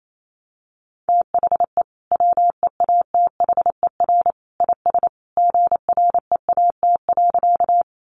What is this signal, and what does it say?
Morse code, heard as a keyed tone, about 28 wpm, 700 Hz. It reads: THE WEATHER IS GREAT.
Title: Sentences from Top 800 Words 28wpm